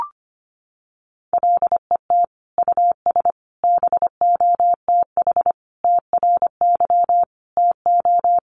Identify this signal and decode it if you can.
Morse code, heard as a keyed tone, about 25 wpm, 700 Hz. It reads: LET US BOTH TRY TO
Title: Sentences from Top 1200 Words 25wpm